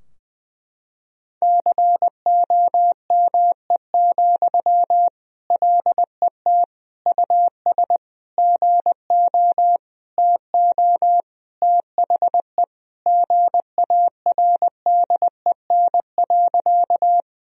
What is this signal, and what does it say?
Morse code, heard as a keyed tone, about 20 wpm, 700 Hz. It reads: COME, LET US GO TO THE GARDEN.
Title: Sentences from Top 1000 Words - Review Part 4 - 20wpm